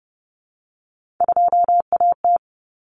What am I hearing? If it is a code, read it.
Morse code, heard as a keyed tone, about 29 wpm, 700 Hz. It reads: 2AT
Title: Common Cut Numbers - 3-Digit 30wpm